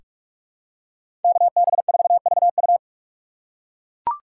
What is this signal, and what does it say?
Morse code, heard as a keyed tone, about 42 wpm, 700 Hz. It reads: KB4VU
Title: K1USN SST QSO Practice - 45wpm